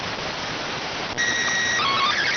SSTV (slow-scan television), PD 290.